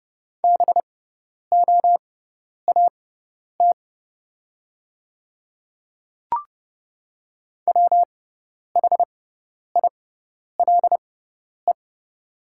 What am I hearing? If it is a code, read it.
Morse code, heard as a keyed tone, about 30 wpm, spaced as for 10 wpm, 700 Hz. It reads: BOAT WHILE
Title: 6x Character Spacing - Top 500 Words - 30wpm